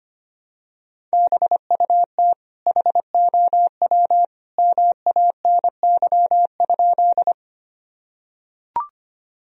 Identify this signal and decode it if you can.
Morse code, heard as a keyed tone, about 25 wpm, 700 Hz. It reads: BUT HOW MANY?
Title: Sentences from Top 1000 Words - Review Part 1 - 25wpm